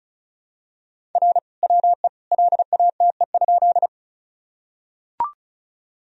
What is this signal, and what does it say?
Morse code, heard as a keyed tone, about 36 wpm, 700 Hz. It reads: R WE LATE?
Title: Sentences from Top 300 Words and Abbreviations 35wpm